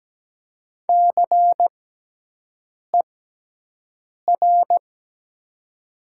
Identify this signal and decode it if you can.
Morse code, heard as a keyed tone, about 17 wpm, spaced as for 6 wpm, 700 Hz. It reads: CER
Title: 6x Character Spacing - Top 500 Words - 17wpm